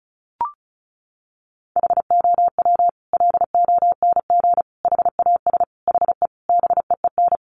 Morse code, heard as a keyed tone, about 34 wpm, 700 Hz. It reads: HOW LONG HAS HE BEEN
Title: Sentences from Top 100 Words 35wpm